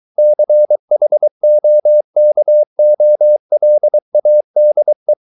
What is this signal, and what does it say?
Morse code, heard as a keyed tone, about 23 wpm, 600 Hz. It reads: CHOKOLADE
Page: Ferie 23 wpm